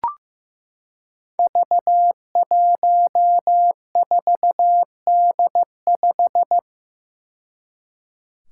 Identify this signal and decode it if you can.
Morse code, heard as a keyed tone, about 15 wpm, 700 Hz. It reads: V14D5
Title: Five Challenging Letter-Number Combinations 15wpm